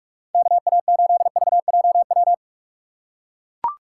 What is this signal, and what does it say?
Morse code, heard as a keyed tone, about 43 wpm, 700 Hz. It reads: KA8VJW
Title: USA Call Signs - 6 Characters - 6x Character Spacing - 45wpm